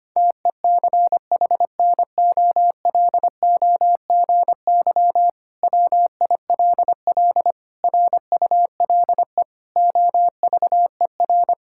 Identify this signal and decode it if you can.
Morse code, heard as a keyed tone, about 25 wpm, 700 Hz. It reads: TECHNOLOGY WILL RULE OVER
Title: Sentences from Top 1000 Words - Review Part 5 - 25wpm